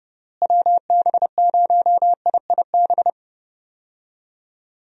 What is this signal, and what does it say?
Morse code, heard as a keyed tone, about 30 wpm, 700 Hz. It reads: WB0IIB
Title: USA Call Signs - 6 Characters - Repeated 3x - 30wpm